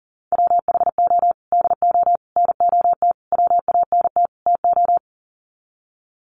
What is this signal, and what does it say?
Morse code, heard as a keyed tone, about 40 wpm, 700 Hz. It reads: WHO DO NOT WANT TO
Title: Sets of 5 Words Encore2 40wpm